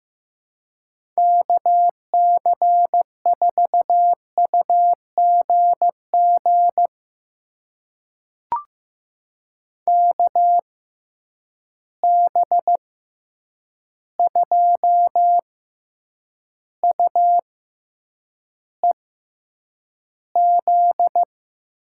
Morse code, heard as a keyed tone, about 15 wpm, spaced as for 8 wpm, 700 Hz. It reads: KC4UGG KB2UEZ
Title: USA Call Signs - 6 Characters - 6x Character Spacing - 15wpm